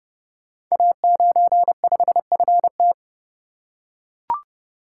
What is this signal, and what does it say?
Morse code, heard as a keyed tone, about 30 wpm, 700 Hz. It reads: A95FT
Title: Code Groups - 5 Letter:Numbers - 30wpm